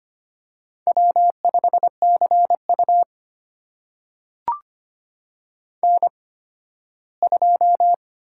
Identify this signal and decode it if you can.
Morse code, heard as a keyed tone, about 25 wpm, spaced as for 15 wpm, 700 Hz. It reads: W5CU N2
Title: USA Call Signs - 4 Characters - 8x Character Spacing - 25wpm